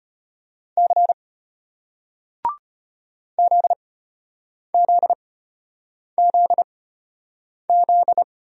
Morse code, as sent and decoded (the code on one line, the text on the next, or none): -.-. / --.. --.. --.. --..
C ZZZZ